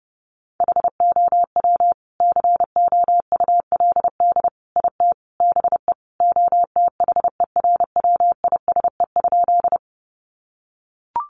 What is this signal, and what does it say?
Morse code, heard as a keyed tone, about 30 wpm, 700 Hz. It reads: HOW COULD IT BE OTHERWISE?
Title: Sentences from Top 1200 Words 30wpm